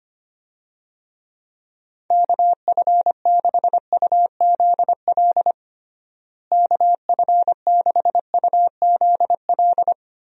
Morse code, heard as a keyed tone, about 25 wpm, 700 Hz. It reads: KF6UZL KF6UZL
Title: USA Call Signs - 6 Characters - Repeated 2x - 25wpm